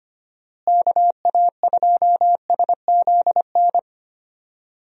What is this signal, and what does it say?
Morse code, heard as a keyed tone, about 25 wpm, 700 Hz. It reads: KA2SZN
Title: USA Call Signs - 6 Characters - Repeated 3x - 25wpm